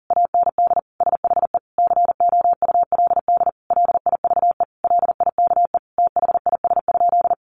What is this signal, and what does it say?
Morse code, heard as a keyed tone, about 40 wpm, 700 Hz. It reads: AND SHE COULD LIVE LIKE THIS?
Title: Sentences from Top 1400 Words 40wpm